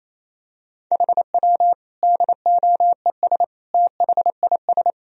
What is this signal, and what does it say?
Morse code, heard as a keyed tone, about 28 wpm, 700 Hz. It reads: HW DOES THIS
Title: Sentences from Top 400 Words and Abbreviations 28wpm